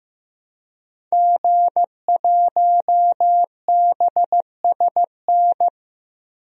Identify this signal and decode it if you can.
Morse code, heard as a keyed tone, about 15 wpm, 700 Hz. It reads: G1BSN
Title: DXCC-100 Call Signs - 5 Characters - 15wpm